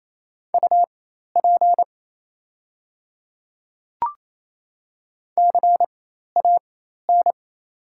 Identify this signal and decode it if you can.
Morse code, heard as a keyed tone, about 27 wpm, spaced as for 13 wpm, 690 Hz. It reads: UP CAN
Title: 4x Character Spacing - Top 100 Words - 28wpm